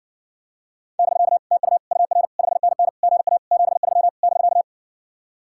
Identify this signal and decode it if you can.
Morse code, heard as a keyed tone, about 60 wpm, 690 Hz. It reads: <BK> TU UR 5NN GA 73 <BK>